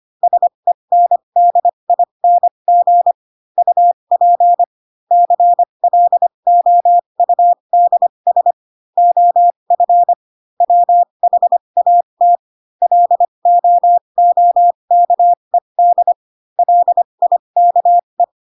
Morse code, heard as a keyed tone, about 25 wpm, spaced as for 22 wpm, 700 Hz. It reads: SENDING UP CLOUDS OF WHAT LOOKED LIKE
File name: Pinocchio / Pinocchio ch3 18 25 ik1hge